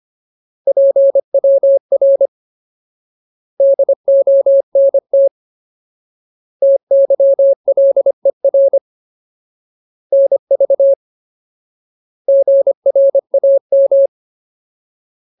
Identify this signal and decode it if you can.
Morse code, heard as a keyed tone, about 25 wpm, 550 Hz. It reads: PWR DONT TYLER NV GRAM
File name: Ultimate Word List / 25WPM Ultimate